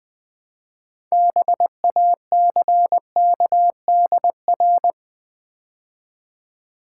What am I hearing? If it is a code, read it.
Morse code, heard as a keyed tone, about 20 wpm, 700 Hz. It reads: BACKDR
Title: Build A Word - 3 Letters At A Time - 8-Letter Words - 20wpm